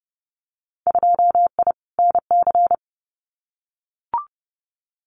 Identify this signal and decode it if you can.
Morse code, heard as a keyed tone, about 30 wpm, 700 Hz. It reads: 2I NC
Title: Winter Field Day Exchanges 30wpm